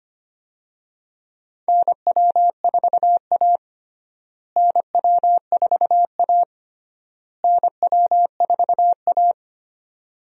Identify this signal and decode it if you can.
Morse code, heard as a keyed tone, about 25 wpm, 700 Hz. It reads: NW4A NW4A NW4A
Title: USA Call Signs - 4 Characters - Repeated 3x - 25wpm